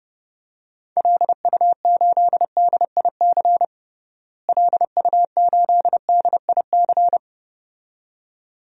Morse code, heard as a keyed tone, about 32 wpm, 700 Hz. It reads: LU8DIC LU8DIC
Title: DXCC-100 Call Signs - 6 Characters - Repeated 2x - 30wpm